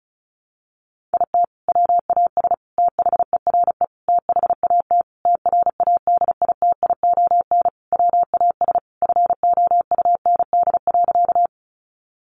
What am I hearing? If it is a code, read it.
Morse code, heard as a keyed tone, about 34 wpm, 700 Hz. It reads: IT WAS THERE THAT TRADITION WAS FOUND.
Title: Sentences from Top 1500 Words 35wpm